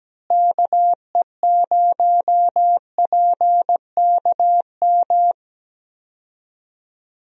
Morse code, heard as a keyed tone, about 17 wpm, 700 Hz. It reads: KE0PKM
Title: USA Call Signs - 6 Characters - Repeated 3x - 17wpm